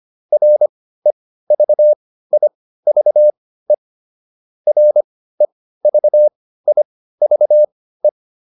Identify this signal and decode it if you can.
Morse code, heard as a keyed tone, about 25 wpm, spaced as for 15 wpm, 600 Hz. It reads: REVIVE REVIVE